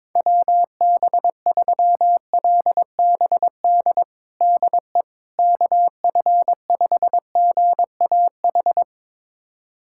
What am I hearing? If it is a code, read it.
Morse code, heard as a keyed tone, about 21 wpm, 700 Hz. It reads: WB3LBD DE KF5GAH
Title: USA Callsign DE Callsign 22wpm